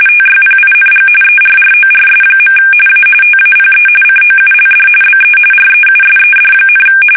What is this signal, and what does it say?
RTTY, about 45.45 baud. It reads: TYPING ERRORS. A TAPE COULD BE REUSED,
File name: fsk850 / teletype-RTTY